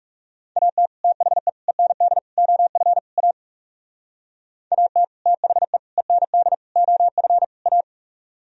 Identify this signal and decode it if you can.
Morse code, heard as a keyed tone, about 36 wpm, 700 Hz. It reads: AT THE END OF A AT THE END OF A
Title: Speed-Racing - Sets of 5 Words 30wpm